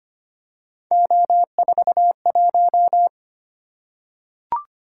Morse code, heard as a keyed tone, about 25 wpm, 700 Hz. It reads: O41
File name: Three Letter-Number Combinations - 25wpm